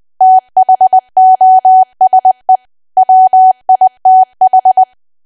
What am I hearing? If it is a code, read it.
Morse code, heard as a keyed tone, about 20 wpm, 750 Hz. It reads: THOSE WITH
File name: cwa / W1AW-Level3 / 20WPM